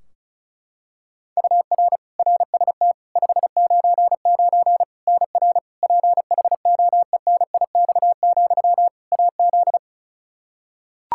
Morse code, heard as a keyed tone, about 36 wpm, 700 Hz. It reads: UR RST 599 NR PHOENIX, AZ
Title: Practice QSO - Exchange 1 - Beginning - 35wpm